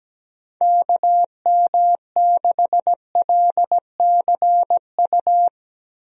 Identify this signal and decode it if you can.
Morse code, heard as a keyed tone, about 17 wpm, 700 Hz. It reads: KM6LCU
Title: USA Call Signs - 6 Characters - Repeated 3x - 17wpm